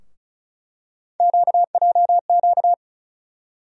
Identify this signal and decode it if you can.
Morse code, heard as a keyed tone, about 34 wpm, 700 Hz. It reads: QJQ
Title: USA Call Sign Postfixes - 3 Letters 35wpm